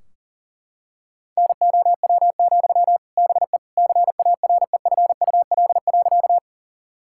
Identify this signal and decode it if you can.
Morse code, heard as a keyed tone, about 37 wpm, 700 Hz. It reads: NOW, BE CAREFUL.
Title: Sentences from Top 1600 Words 40wpm